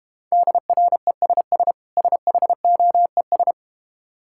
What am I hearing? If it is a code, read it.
Morse code, heard as a keyed tone, about 32 wpm, 700 Hz. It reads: DRESS SHOES